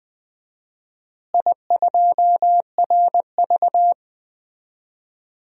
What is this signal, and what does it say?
Morse code, heard as a keyed tone, about 20 wpm, 700 Hz. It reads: I2RV
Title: DXCC-100 Call Signs - 4 Characters - Repeated 2x - 20wpm